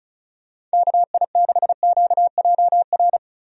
Morse code, heard as a keyed tone, about 36 wpm, 700 Hz. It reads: KI6QJR
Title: USA Call Signs - 6 Characters - 35wpm